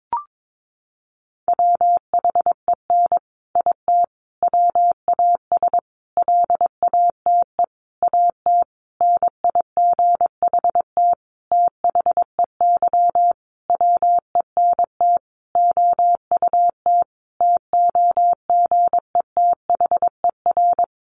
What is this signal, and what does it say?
Morse code, heard as a keyed tone, about 22 wpm, 700 Hz. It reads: WHEN IT WAS LATE AT NIGHT THEY WENT OUT TOGETHER
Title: Sentences from Top 600 Words 22wpm